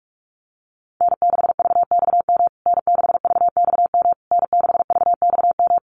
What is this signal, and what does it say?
Morse code, heard as a keyed tone, about 44 wpm, 700 Hz. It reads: N6VXM N6VXM N6VXM
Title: USA Call Signs Repeated 3x 45wpm